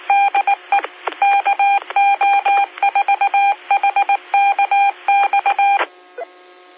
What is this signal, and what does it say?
Morse code, heard as a keyed tone, about 19 wpm, 800 Hz. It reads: DE KO4HKX